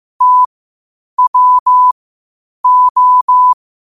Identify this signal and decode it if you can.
Morse code, heard as a keyed tone, about 15 wpm, spaced as for 8 wpm, 1000 Hz. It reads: TWO
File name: qotdfiles / 8wpm